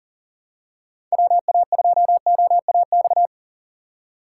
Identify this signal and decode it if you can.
Morse code, heard as a keyed tone, about 38 wpm, 700 Hz. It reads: WA2OAX
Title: USA Call Signs - 6 Characters - 3x Character Spacing - 40wpm